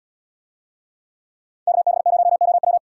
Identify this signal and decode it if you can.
Morse code, heard as a keyed tone, about 76 wpm, 700 Hz. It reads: KB0OW